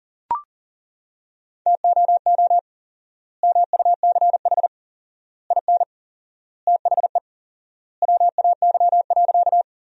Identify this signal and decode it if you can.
Morse code, heard as a keyed tone, about 40 wpm, 700 Hz. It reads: TOO MUCH IN THE WAY.
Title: 4x Word-Spacing - Sentences from Top 200 Words - 40wpm